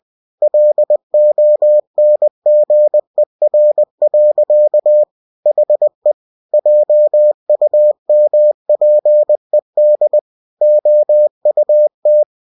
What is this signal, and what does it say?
Morse code, heard as a keyed tone, about 20 wpm, 600 Hz. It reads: LONGER. HE JUMPED OUT
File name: cw-books / winnie-the-pooh / winnie-20wpm-page-0040